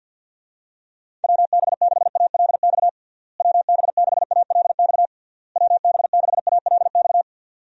Morse code, heard as a keyed tone, about 52 wpm, 700 Hz. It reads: WB6ALX WB6ALX WB6ALX
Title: USA Call Signs - 6 Characters - Repeated 3x - 50wpm